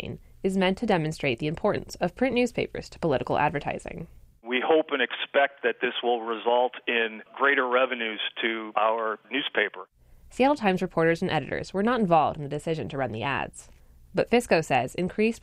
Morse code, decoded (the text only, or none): none